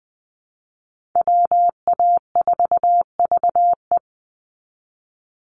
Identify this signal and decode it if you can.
Morse code, heard as a keyed tone, about 20 wpm, 700 Hz. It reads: WA4VE
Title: USA Call Signs 20wpm